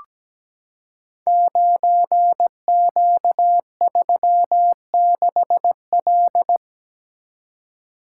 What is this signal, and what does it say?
Morse code, heard as a keyed tone, about 17 wpm, 700 Hz. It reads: 9Q36L
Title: Code Groups - 5 Letter:Numbers - 17wpm